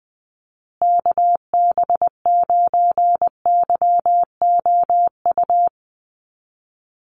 Morse code, heard as a keyed tone, about 20 wpm, 700 Hz. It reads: KB9YOU
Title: USA Call Signs Repeated 2x 20wpm